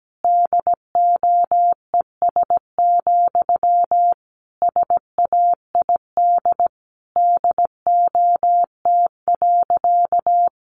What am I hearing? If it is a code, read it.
Morse code, heard as a keyed tone, about 17 wpm, 700 Hz. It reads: DOES, SAID DOT.
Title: Sentences from Top 1500 Words 17wpm